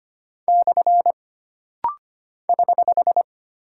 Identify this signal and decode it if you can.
Morse code, heard as a keyed tone, about 25 wpm, 700 Hz. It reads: / <HH>
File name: Single Punctuation-Prosign - Rapid-Fire - 25wpm